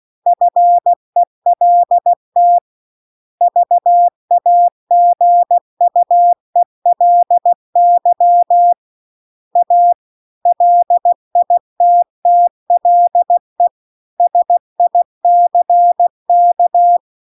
Morse code, heard as a keyed tone, about 16 wpm, 700 Hz. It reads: FELT VAGUELY A LITTLE SICK
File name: CW-The-Big-Twoheaded-River-2-16WPM